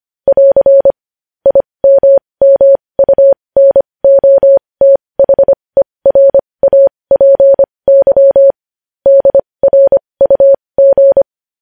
Morse code, heard as a keyed tone, about 25 wpm, spaced as for 20 wpm, 550 Hz. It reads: <AR> IMMUNOTHERAPY DRUG
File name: podcast / News headlines-1x-25x20